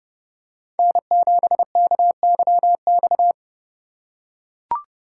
Morse code, heard as a keyed tone, about 29 wpm, 700 Hz. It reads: N7KYX